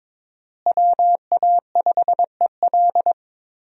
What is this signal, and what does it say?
Morse code, heard as a keyed tone, about 21 wpm, 700 Hz. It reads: WA5EL
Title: USA Call Signs - 5 Characters - Repeated 3x - 22wpm